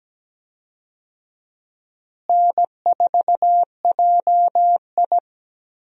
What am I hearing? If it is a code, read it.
Morse code, heard as a keyed tone, about 17 wpm, 700 Hz. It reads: N4JI
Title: USA Call Signs - 4 Characters - Repeated 2x - 17wpm